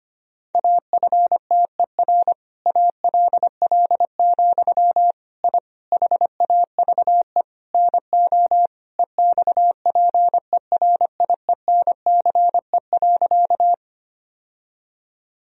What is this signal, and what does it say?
Morse code, heard as a keyed tone, about 25 wpm, 700 Hz. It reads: AFTER ALL, I HAVE NO EXPERIENCE.
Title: Sentences from Top 1000 Words - Review Part 3 - 25wpm